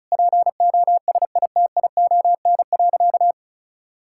Morse code, heard as a keyed tone, about 34 wpm, 700 Hz. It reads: POSITION.
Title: Sentences from Top 1000 Words - Review Part 5 - 35wpm